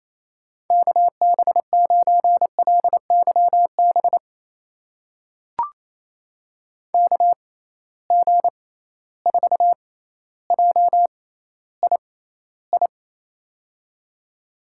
Morse code, heard as a keyed tone, about 28 wpm, spaced as for 14 wpm, 700 Hz. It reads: KB9LYB KG4JII
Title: USA Call Signs - 6 Characters - 6x Character Spacing - 28wpm